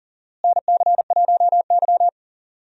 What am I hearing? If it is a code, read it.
Morse code, heard as a keyed tone, about 37 wpm, 700 Hz. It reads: NC1Y
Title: USA Call Signs - 4 Characters - Repeated 2x - 40wpm